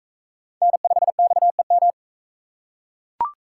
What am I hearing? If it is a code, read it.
Morse code, heard as a keyed tone, about 41 wpm, 700 Hz. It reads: N5XEM